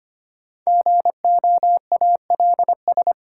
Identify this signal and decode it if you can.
Morse code, heard as a keyed tone, about 25 wpm, 700 Hz. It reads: GOALS
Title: Sentences from Top 1000 Words - Review Part 6 - New Word Drill - 25wpm